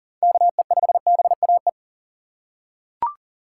Morse code, heard as a keyed tone, about 40 wpm, 700 Hz. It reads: KE5BAE
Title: USA Call Signs - 4x Character Spacing - 40wpm